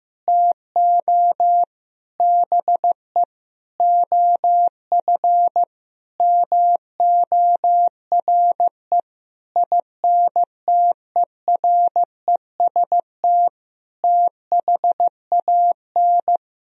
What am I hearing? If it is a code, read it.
Morse code, heard as a keyed tone, about 15 wpm, 700 Hz. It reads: TO BE OF MORE INTEREST THAN
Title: Sentences from Top 1000 Words - Review Part 2 - 15wpm